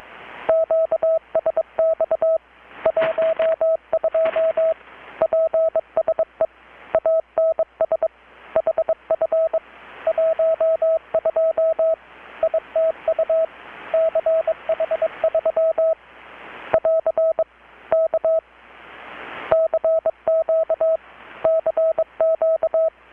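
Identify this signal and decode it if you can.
Morse code, heard as a keyed tone, about 22 wpm, 650 Hz, with radio noise. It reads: QSX 12 PSE ANS HF 12 ITU CH3 <AR> K CQ CQ
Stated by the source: Morse from KPH's Night of Nights XX maritime radiotelegraphy event, received in Chiapas, Mexico about 3,800 km away on a Kenwood R-600 with a grounded 10 m long-wire antenna.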